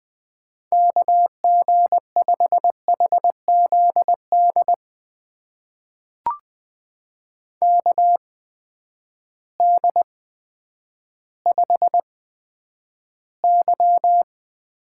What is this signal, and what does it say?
Morse code, heard as a keyed tone, about 20 wpm, spaced as for 10 wpm, 700 Hz. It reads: KG5HZD KD5Y
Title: USA Call Signs - 6 Characters - 8x Character Spacing - 20wpm